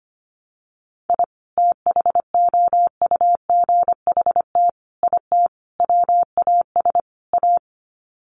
Morse code, heard as a keyed tone, about 25 wpm, 700 Hz. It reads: I THOUGHT IT WAS A
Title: Sets of 5 Words 25wpm